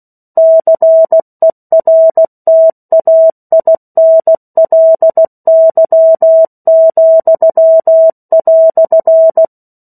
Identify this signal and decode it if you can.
Morse code, heard as a keyed tone, about 16 wpm, 650 Hz. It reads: CERTAINLY,"
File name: 03-Chapter-03 16 WPM